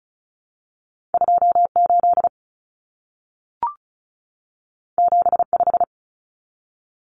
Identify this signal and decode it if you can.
Morse code, heard as a keyed tone, about 36 wpm, 700 Hz. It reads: 28 75
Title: Common Cut Numbers - 2-Digit 35wpm